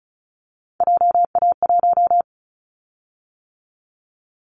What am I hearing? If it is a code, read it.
Morse code, heard as a keyed tone, about 36 wpm, 700 Hz. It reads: JA1
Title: Callsign Prefixes for Top 50 DXCC - 35wpm